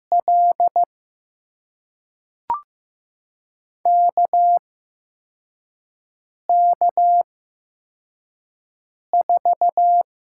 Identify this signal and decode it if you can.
Morse code, heard as a keyed tone, about 16 wpm, spaced as for 4 wpm, 700 Hz. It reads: L KK4
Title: USA Call Signs - 6 Characters - 8x Character Spacing - 15wpm